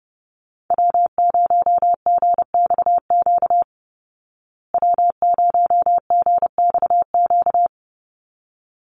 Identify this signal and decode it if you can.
Morse code, heard as a keyed tone, about 30 wpm, 700 Hz. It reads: W0GXQ W0GXQ
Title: Call Signs Repeated 3x - 30wpm